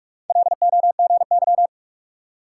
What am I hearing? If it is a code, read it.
Morse code, heard as a keyed tone, about 57 wpm, 700 Hz. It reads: LOGY